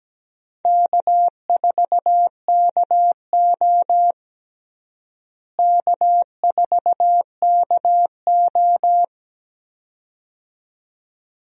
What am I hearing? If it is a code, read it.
Morse code, heard as a keyed tone, about 17 wpm, 700 Hz. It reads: K4KO K4KO
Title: USA Call Signs - 4 Characters - Repeated 2x - 17wpm